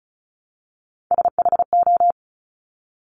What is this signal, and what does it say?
Morse code, heard as a keyed tone, about 36 wpm, 700 Hz. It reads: SHO
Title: Top Three Letter Combinations in Words 35wpm